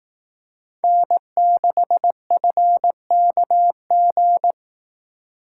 Morse code, heard as a keyed tone, about 18 wpm, 700 Hz. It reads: N6FKG
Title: Speed Racing - USA 1x3 Call Signs 15wpm